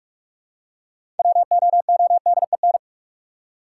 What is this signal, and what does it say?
Morse code, heard as a keyed tone, about 43 wpm, 700 Hz. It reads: WOODEN
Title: Six Letter Words - 45wpm